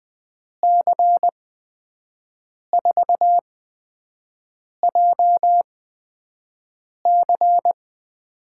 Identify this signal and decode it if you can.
Morse code, heard as a keyed tone, about 20 wpm, spaced as for 5 wpm, 700 Hz. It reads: C4JC
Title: USA Call Signs - 8x Character Spacing - 20wpm